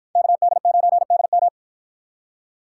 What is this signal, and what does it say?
Morse code, heard as a keyed tone, about 53 wpm, 700 Hz. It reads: KD9DM